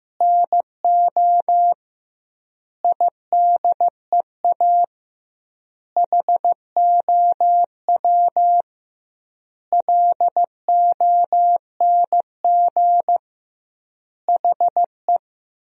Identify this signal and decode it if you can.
Morse code, heard as a keyed tone, about 15 wpm, 700 Hz. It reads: NO IDEA HOW LONG HE
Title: 2x Word-Spacing - Sentences from Top 300 Words - 15wpm